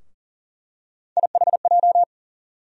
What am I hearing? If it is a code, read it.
Morse code, heard as a keyed tone, about 40 wpm, 700 Hz. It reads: IHJ
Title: USA Call Sign Postfixes - 3 Letters 40wpm